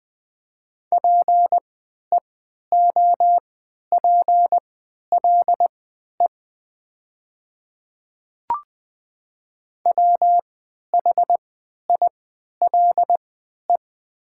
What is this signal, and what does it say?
Morse code, heard as a keyed tone, about 20 wpm, spaced as for 11 wpm, 700 Hz. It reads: PEOPLE WHILE
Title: 3x Character Spacing - Top 300 Words - 20wpm